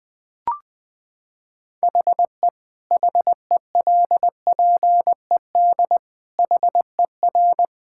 Morse code, heard as a keyed tone, about 20 wpm, 700 Hz. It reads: HE HELPED HER
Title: Sentences with 8-Word Phrases - 20wpm